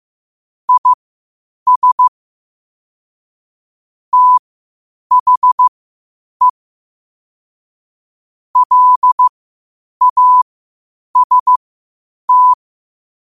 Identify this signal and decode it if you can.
Morse code, heard as a keyed tone, about 15 wpm, spaced as for 8 wpm, 1000 Hz. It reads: IS THE LAST
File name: qotdfiles / 8wpm